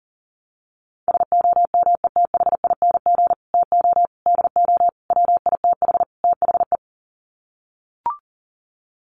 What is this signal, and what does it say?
Morse code, heard as a keyed tone, about 40 wpm, 700 Hz. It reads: SOMETHING TO DO WITH THE
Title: Sets of 5 Words Encore 40wpm